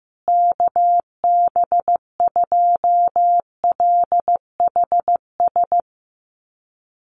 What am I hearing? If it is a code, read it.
Morse code, heard as a keyed tone, about 15 wpm, 700 Hz. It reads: KB2LHS
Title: USA Call Signs Repeated 3x 15wpm